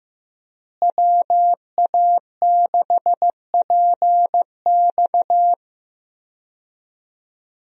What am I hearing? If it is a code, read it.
Morse code, heard as a keyed tone, about 15 wpm, 700 Hz. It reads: WA6PX
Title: USA Call Signs - 5 Characters - Repeated 3x - 15wpm